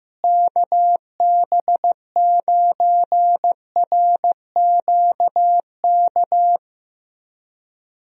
Morse code, heard as a keyed tone, about 15 wpm, 700 Hz. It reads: KB9RQK
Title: USA Call Signs - 6 Characters - Repeated 2x - 15wpm